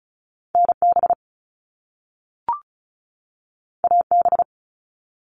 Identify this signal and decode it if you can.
Morse code, heard as a keyed tone, about 36 wpm, 700 Hz. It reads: NB AB
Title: Canadian Provinces and Territories 35wpm